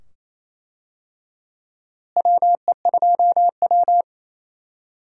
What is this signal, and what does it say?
Morse code, heard as a keyed tone, about 28 wpm, 700 Hz. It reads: WE2W
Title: USA Call Signs - 4 Characters - Repeated 2x - 28wpm